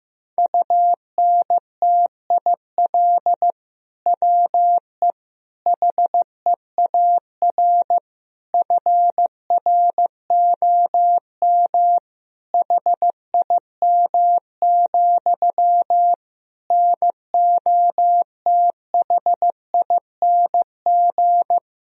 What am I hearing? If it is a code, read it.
Morse code, heard as a keyed tone, about 15 wpm, 700 Hz. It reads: UNTIL WE HEAR FROM HIM, NOTHING
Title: Sentences from Top 1000 Words - Review Part 1 - 15wpm